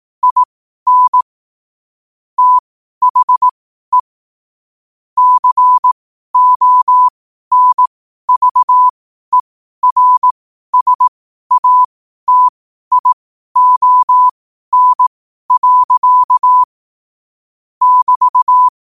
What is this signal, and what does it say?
Morse code, heard as a keyed tone, about 18 wpm, spaced as for 12 wpm, 1000 Hz. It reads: IN THE CONVERSATION. <BT>
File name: qotdfiles / 12wpm